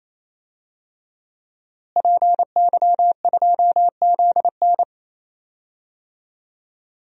Morse code, heard as a keyed tone, about 27 wpm, 700 Hz. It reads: PY2ZN
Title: DXCC-100 Call Signs - 5 Characters - Repeated 3x - 28wpm